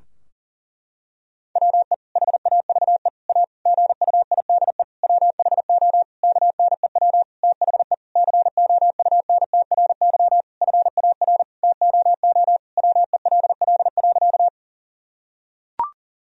Morse code, heard as a keyed tone, about 40 wpm, 700 Hz. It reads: WE HAVE A GUIDE WHO KNEW THE COUNTRY FAR TOO WELL.
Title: Sentences from Top 1000 Words - Review Part 3 - 40wpm